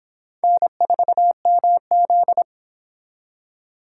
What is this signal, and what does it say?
Morse code, heard as a keyed tone, about 26 wpm, 700 Hz. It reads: N4MZ